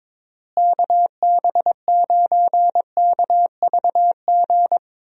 Morse code, heard as a keyed tone, about 22 wpm, 700 Hz. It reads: KB9KVG
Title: USA Call Signs - 6 Characters - Repeated 3x - 22wpm